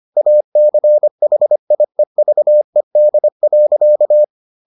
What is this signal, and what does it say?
Morse code, heard as a keyed tone, about 25 wpm, 600 Hz. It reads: ACHIEVED.